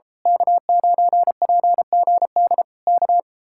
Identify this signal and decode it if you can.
Morse code, heard as a keyed tone, about 27 wpm, 700 Hz. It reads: K9PGD K
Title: Speed-Racing Common QSO Phrases - 28wpm